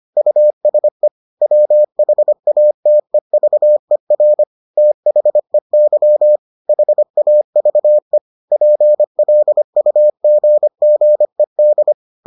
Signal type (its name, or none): Morse code